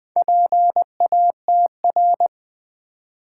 Morse code, heard as a keyed tone, about 20 wpm, 700 Hz. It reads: PATR
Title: Build A Word - 2 Letters At A Time - 6-Letter Words - 20wpm